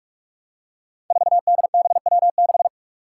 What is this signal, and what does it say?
Morse code, heard as a keyed tone, about 46 wpm, 700 Hz. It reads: 4DBW6
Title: Five Challenging Letter-Number Combinations 45wpm